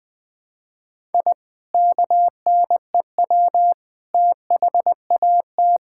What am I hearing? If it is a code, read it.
Morse code, heard as a keyed tone, about 21 wpm, 700 Hz. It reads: I KNEW THAT
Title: Sentences with 3-Word Phrases - 20wpm